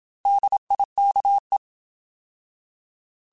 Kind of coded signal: Morse code